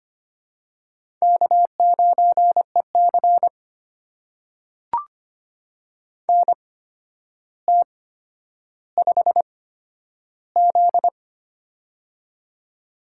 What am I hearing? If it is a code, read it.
Morse code, heard as a keyed tone, about 25 wpm, spaced as for 11 wpm, 700 Hz. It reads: K9EC NT5Z
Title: USA Call Signs - 4 Characters - 8x Character Spacing - 25wpm